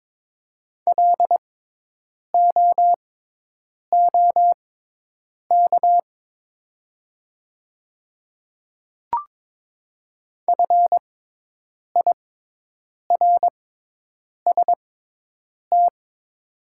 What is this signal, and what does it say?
Morse code, heard as a keyed tone, about 22 wpm, spaced as for 8 wpm, 700 Hz. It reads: LOOK FIRST
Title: 6x Character Spacing - Top 100 Words - 22wpm